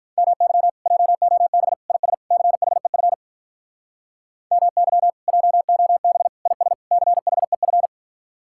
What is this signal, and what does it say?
Morse code, heard as a keyed tone, about 44 wpm, 700 Hz. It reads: MY JOB IS CHEF MY JOB IS CHEF